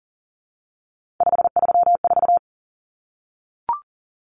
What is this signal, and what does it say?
Morse code, heard as a keyed tone, about 40 wpm, 700 Hz. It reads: 534
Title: Common Cut Numbers - 3-Digit 40wpm